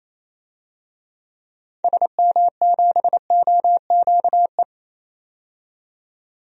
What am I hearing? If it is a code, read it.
Morse code, heard as a keyed tone, about 28 wpm, 700 Hz. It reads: SM7OQE
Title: DXCC-100 Call Signs - 6 Characters - 28wpm